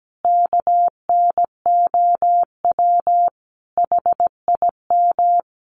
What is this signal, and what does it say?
Morse code, heard as a keyed tone, about 17 wpm, 700 Hz. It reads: KNOW HIM
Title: Sentences from Top 1400 Words 17wpm